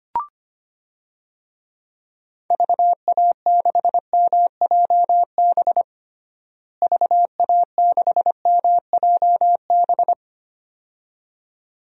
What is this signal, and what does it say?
Morse code, heard as a keyed tone, about 25 wpm, 700 Hz. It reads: VA6MJB VA6MJB
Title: DXCC-100 Call Signs - 6 Characters - Repeated 2x - 25wpm